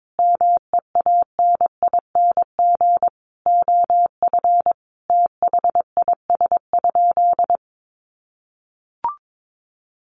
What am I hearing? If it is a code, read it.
Morse code, heard as a keyed tone, about 22 wpm, 700 Hz. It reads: MEANING OF THIS?
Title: Sentences from Top 900 Words 22wpm